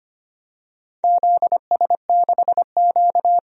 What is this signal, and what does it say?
Morse code, heard as a keyed tone, about 25 wpm, 700 Hz. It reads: ZS6Q
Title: DXCC-100 Call Signs - 4 Characters - 25wpm